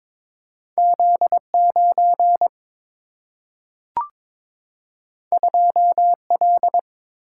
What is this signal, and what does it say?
Morse code, heard as a keyed tone, about 21 wpm, 700 Hz. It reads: Z9 2L
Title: Single-Letter Single-Number 22wpm